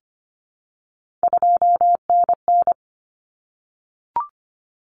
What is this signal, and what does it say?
Morse code, heard as a keyed tone, about 25 wpm, 700 Hz. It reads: 2NN
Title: Common Cut Numbers - 3-Digit 25wpm